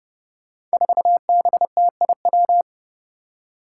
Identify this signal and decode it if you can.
Morse code, heard as a keyed tone, about 30 wpm, 700 Hz. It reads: 4BTIW
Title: Five Challenging Letter-Number Combinations 30wpm